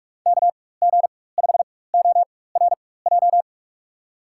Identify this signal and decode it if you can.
Morse code, heard as a keyed tone, about 46 wpm, spaced as for 21 wpm, 700 Hz. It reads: KG5ORJ
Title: USA Call Signs - 6 Characters - 4x Character Spacing - 45wpm